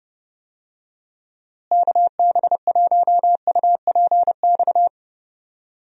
Morse code, heard as a keyed tone, about 30 wpm, 700 Hz. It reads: KB1UPX
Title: USA Call Signs - 6 Characters - Repeated 3x - 30wpm